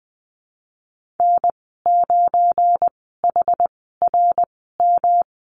Morse code, heard as a keyed tone, about 20 wpm, spaced as for 15 wpm, 700 Hz. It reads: N9HRM
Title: 173 Lesson 28g - Number 9 - Call Signs 20wpm